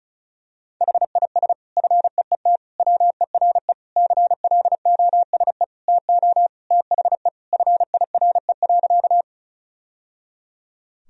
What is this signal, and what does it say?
Morse code, heard as a keyed tone, about 34 wpm, 700 Hz. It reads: HIS FEET WERE CLOSE TO THE FIRE.
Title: Sentences from Top 1000 Words - Review Part 1 - 35wpm